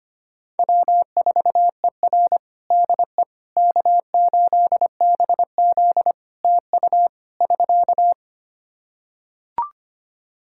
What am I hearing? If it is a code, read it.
Morse code, heard as a keyed tone, about 25 wpm, 700 Hz. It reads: W4ER DE K8BZ TU <SK>
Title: Practice QSO - Exchange 4 - Ending - 25wpm